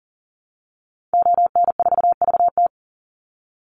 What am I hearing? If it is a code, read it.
Morse code, heard as a keyed tone, about 40 wpm, 700 Hz. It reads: ON4VT
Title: Call Signs - 40wpm